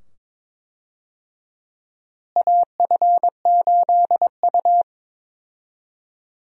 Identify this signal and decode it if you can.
Morse code, heard as a keyed tone, about 21 wpm, 700 Hz. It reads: AF8U
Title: USA Call Signs - 4 Characters - Repeated 2x - 22wpm